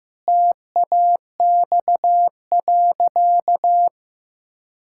Morse code, heard as a keyed tone, about 15 wpm, 700 Hz. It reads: TAX.
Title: Sentences from Top 1000 Words - Review Part 5 - 15wpm